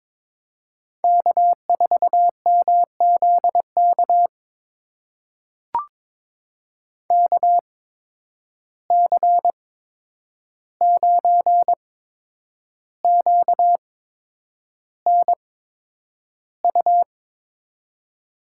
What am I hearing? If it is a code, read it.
Morse code, heard as a keyed tone, about 22 wpm, spaced as for 9 wpm, 700 Hz. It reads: K4MZK KC9QNU
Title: USA Call Signs - 8x Character Spacing - 22wpm